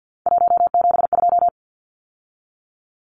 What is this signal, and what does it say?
Morse code, heard as a keyed tone, about 50 wpm, 700 Hz. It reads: 172
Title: Numbers - 3-Digit 50wpm